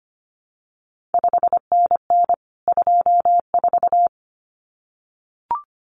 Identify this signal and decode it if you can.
Morse code, heard as a keyed tone, about 25 wpm, 700 Hz. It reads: 5NN 24
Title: CQ WW DX Contest Exchange Encore - 25wpm